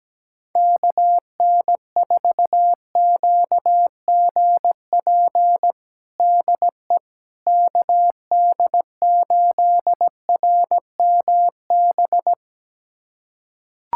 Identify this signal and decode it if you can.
Morse code, heard as a keyed tone, about 17 wpm, 700 Hz. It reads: KN4QGP DE KD8RMB
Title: USA Callsign DE Callsign 17wpm